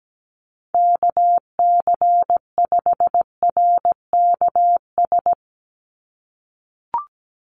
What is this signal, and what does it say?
Morse code, heard as a keyed tone, about 17 wpm, 700 Hz. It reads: KC5RKS
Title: USA Call Signs 17wpm